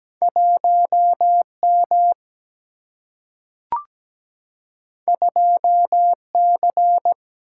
Morse code, heard as a keyed tone, about 17 wpm, 700 Hz. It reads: 1M 2C
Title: Single-Letter Single-Number 17wpm